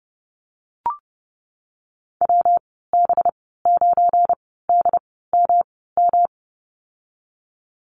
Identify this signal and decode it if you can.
Morse code, heard as a keyed tone, about 30 wpm, spaced as for 17 wpm, 700 Hz. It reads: WB9DMM
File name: Lesson 37g - Letter x - Call Signs 30wpm